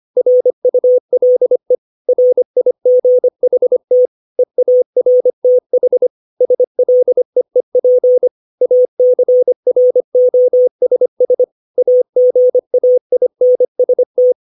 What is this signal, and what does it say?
Morse code, heard as a keyed tone, about 25 wpm, 500 Hz. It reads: RULE RIGHT EARTH SLEEP ACROSS AGAINST
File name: 632 words / 632 Words 25WPM